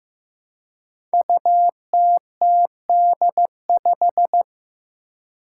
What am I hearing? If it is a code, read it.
Morse code, heard as a keyed tone, about 16 wpm, 700 Hz. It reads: UTTD5
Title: Five Challenging Letter-Number Combinations 15wpm